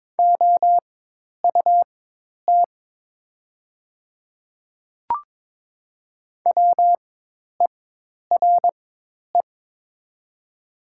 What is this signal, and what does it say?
Morse code, heard as a keyed tone, about 22 wpm, spaced as for 10 wpm, 700 Hz. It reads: OUT WERE
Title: 4x Character Spacing - Top 200 Words - 22wpm